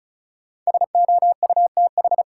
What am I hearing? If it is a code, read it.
Morse code, heard as a keyed tone, about 36 wpm, 700 Hz. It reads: SOUTH
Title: Sentences from Top 400 Words - New Word Drill - 35wpm